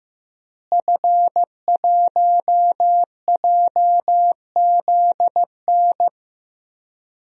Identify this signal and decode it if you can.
Morse code, heard as a keyed tone, about 15 wpm, 700 Hz. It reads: F1JZN
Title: DXCC-100 Call Signs - 5 Characters - Repeated 2x - 15wpm